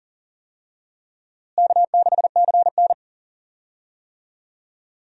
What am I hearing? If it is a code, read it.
Morse code, heard as a keyed tone, about 39 wpm, 700 Hz. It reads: K6CN
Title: USA Call Signs - 4 Characters - Repeated 3x - 40wpm